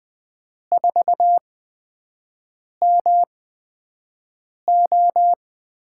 Morse code, heard as a keyed tone, about 20 wpm, spaced as for 5 wpm, 700 Hz. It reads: 4MO
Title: USA Call Signs - 5 Characters - 8x Character Spacing - 20wpm